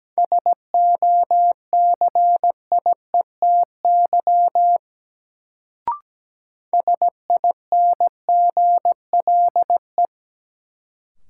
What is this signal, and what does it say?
Morse code, heard as a keyed tone, about 17 wpm, 700 Hz. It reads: SOCIETY SINGLE
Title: Sentences from Top 700 Words - New Word Drill - 17wpm